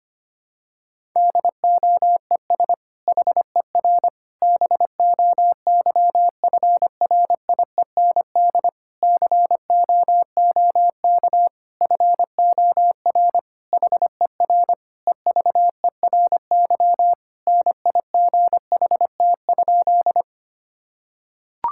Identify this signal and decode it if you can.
Morse code, heard as a keyed tone, about 25 wpm, 700 Hz. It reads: DOES HER BOYFRIEND COOK FOR HER EVERY NIGHT?
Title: Sentences from Top 1000 Words - Review Part 5 - 25wpm